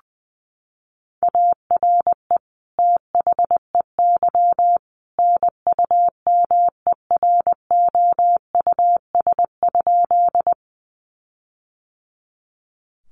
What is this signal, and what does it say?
Morse code, heard as a keyed tone, about 20 wpm, 700 Hz. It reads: ARE THEY NUMEROUS?
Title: Sentences from Top 1400 Words 20wpm